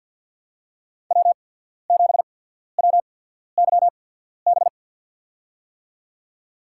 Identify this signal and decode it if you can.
Morse code, heard as a keyed tone, about 52 wpm, spaced as for 14 wpm, 700 Hz. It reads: W7WYB
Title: USA Call Signs - 5 Characters - 8x Character Spacing - 50wpm